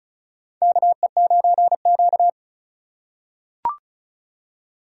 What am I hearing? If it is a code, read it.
Morse code, heard as a keyed tone, about 36 wpm, 700 Hz. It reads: KE9Q
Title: USA Call Signs - 4 Characters - 3x Character Spacing - 35wpm